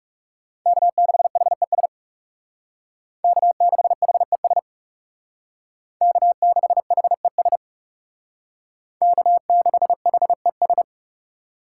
Morse code, heard as a keyed tone, about 38 wpm, 700 Hz. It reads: K6HES K6HES K6HES K6HES